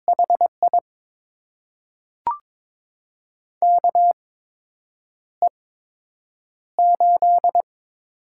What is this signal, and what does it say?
Morse code, heard as a keyed tone, about 22 wpm, spaced as for 8 wpm, 700 Hz. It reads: HI KE8
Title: USA Call Signs - 6 Characters - 8x Character Spacing - 22wpm